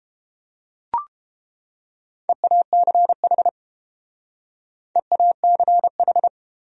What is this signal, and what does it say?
Morse code, heard as a keyed tone, about 28 wpm, 700 Hz. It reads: EACH EACH